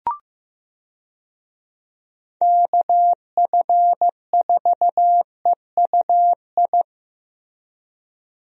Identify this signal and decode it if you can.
Morse code, heard as a keyed tone, about 15 wpm, 700 Hz. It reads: KF4EUI
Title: USA Call Signs - 6 Characters - Repeated 2x - 15wpm